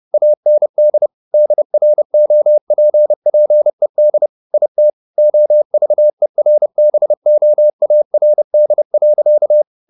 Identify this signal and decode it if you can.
Morse code, heard as a keyed tone, about 30 wpm, 600 Hz. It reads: AND DROPPED IT OVERBOARD.
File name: the old man and the sea / the old man and the sea 1 30wpm